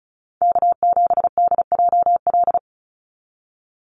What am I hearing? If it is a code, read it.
Morse code, heard as a keyed tone, about 34 wpm, 700 Hz. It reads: K7DJL
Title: NA SOTA Chaser Callsigns 35wpm